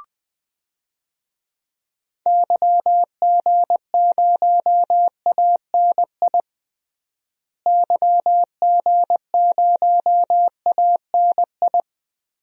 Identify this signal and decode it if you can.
Morse code, heard as a keyed tone, about 21 wpm, 700 Hz. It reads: YG0ANI YG0ANI
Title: DXCC-100 Call Signs - 6 Characters - Repeated 3x - 20wpm